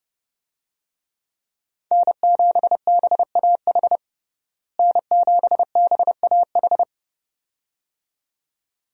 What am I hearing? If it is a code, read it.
Morse code, heard as a keyed tone, about 32 wpm, 700 Hz. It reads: N7BAH N7BAH
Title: USA Call Signs - 5 Characters - Repeated 2x - 30wpm